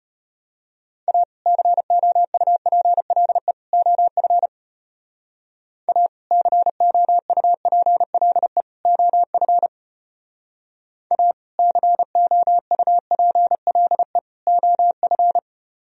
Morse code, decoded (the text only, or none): A COUPLE OF A COUPLE OF A COUPLE OF